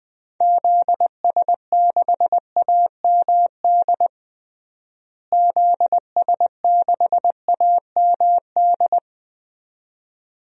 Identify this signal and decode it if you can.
Morse code, heard as a keyed tone, about 20 wpm, 700 Hz. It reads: ZS6AMD ZS6AMD
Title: DXCC-100 Call Signs - 6 Characters - Repeated 2x - 20wpm